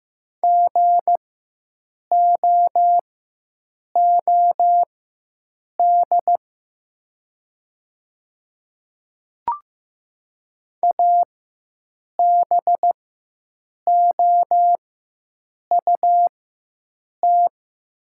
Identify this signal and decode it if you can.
Morse code, heard as a keyed tone, about 15 wpm, spaced as for 7 wpm, 700 Hz. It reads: GOOD ABOUT
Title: 4x Character Spacing - Top 200 Words - 15wpm